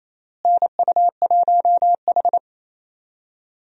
Morse code, heard as a keyed tone, about 27 wpm, 700 Hz. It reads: NU1H
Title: USA Call Signs - 4 Characters - Repeated 3x - 28wpm